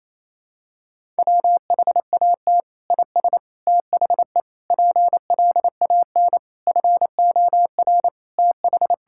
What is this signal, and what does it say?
Morse code, heard as a keyed tone, about 27 wpm, 700 Hz. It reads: WHAT IS THE PLAN FOR TH
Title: Sentences from Top 1000 Words - Review Part 5 - 28wpm